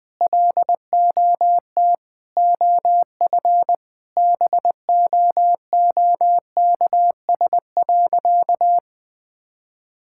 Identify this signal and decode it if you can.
Morse code, heard as a keyed tone, about 20 wpm, 700 Hz. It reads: LOT OF BOOKS.
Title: Sentences with 5-Word Phrases - Spoken Beginning - 20wpm